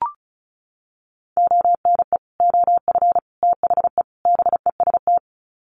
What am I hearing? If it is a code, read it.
Morse code, heard as a keyed tone, about 34 wpm, 700 Hz. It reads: ONE OF THE BEST
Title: Sets of 4 Words 35wpm